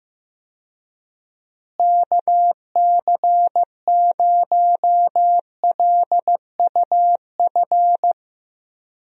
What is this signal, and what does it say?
Morse code, heard as a keyed tone, about 15 wpm, 700 Hz. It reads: KC0LUF
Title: USA Call Signs - 6 Characters - Repeated 2x - 15wpm